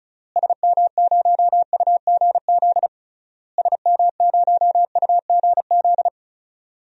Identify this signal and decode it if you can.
Morse code, heard as a keyed tone, about 36 wpm, 700 Hz. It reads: SM0UGZ SM0UGZ
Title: DXCC-100 Call Signs - 6 Characters - Repeated 3x - 35wpm